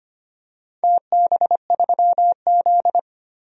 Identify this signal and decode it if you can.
Morse code, heard as a keyed tone, about 25 wpm, 700 Hz. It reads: TB3Z
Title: DXCC-100 Call Signs - 4 Characters - 25wpm